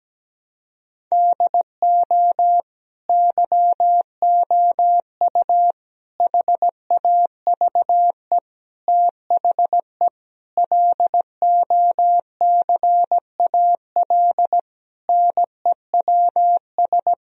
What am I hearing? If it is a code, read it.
Morse code, heard as a keyed tone, about 17 wpm, 700 Hz. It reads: DO YOU HAVE THE LOCAL NEWS
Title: Sentences from Top 1000 Words - Review Part 3 - 17wpm